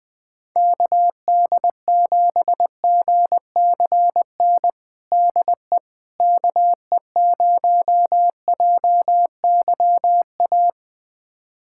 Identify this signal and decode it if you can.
Morse code, heard as a keyed tone, about 21 wpm, 700 Hz. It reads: KD7GCN DE KE0JYA
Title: USA Callsign DE Callsign 20wpm